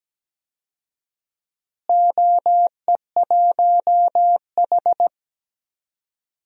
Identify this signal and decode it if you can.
Morse code, heard as a keyed tone, about 17 wpm, 700 Hz. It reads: OE1H
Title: DXCC-100 Call Signs - 4 Characters - Repeated 3x - 17wpm